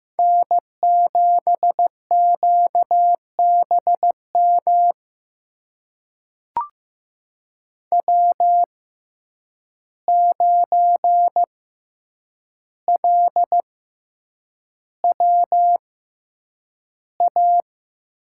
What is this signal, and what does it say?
Morse code, heard as a keyed tone, about 15 wpm, spaced as for 8 wpm, 700 Hz. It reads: N7QBM W9LWA
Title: USA Call Signs - 5 Characters - 6x Character Spacing - 15wpm